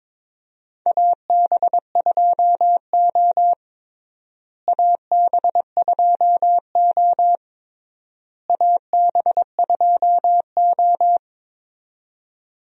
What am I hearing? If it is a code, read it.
Morse code, heard as a keyed tone, about 22 wpm, 700 Hz. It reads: AB2O AB2O AB2O
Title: USA Call Signs - 4 Characters - Repeated 3x - 22wpm